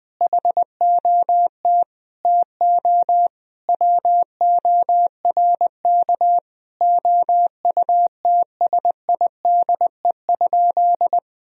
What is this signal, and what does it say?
Morse code, heard as a keyed tone, about 20 wpm, 700 Hz. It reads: HOT TO WORK OUTSIDE?
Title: Sentences from Top 1000 Words - Review Part 4 - 20wpm